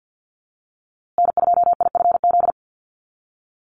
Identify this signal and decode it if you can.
Morse code, heard as a keyed tone, about 48 wpm, 700 Hz. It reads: N2IFZ
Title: USA Call Signs Repeated 3x 50wpm